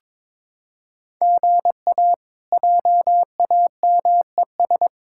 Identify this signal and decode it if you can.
Morse code, heard as a keyed tone, about 22 wpm, 700 Hz. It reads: GA JAMES
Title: K1USN SST QSO Practice - 22wpm